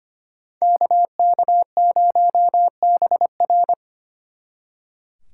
Morse code, heard as a keyed tone, about 28 wpm, 700 Hz. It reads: KK0BR
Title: Speed Racing - USA 2x2 Call Signs 25wpm